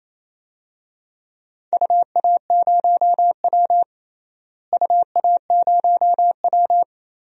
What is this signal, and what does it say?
Morse code, heard as a keyed tone, about 28 wpm, 700 Hz. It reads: UA0W UA0W
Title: DXCC-100 Call Signs - 4 Characters - Repeated 2x - 28wpm